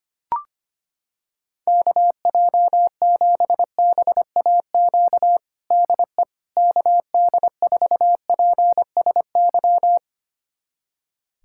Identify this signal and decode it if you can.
Morse code, heard as a keyed tone, about 25 wpm, 700 Hz. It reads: KJ7BAQ DE KD4PSY
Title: USA Callsign DE Callsign 25wpm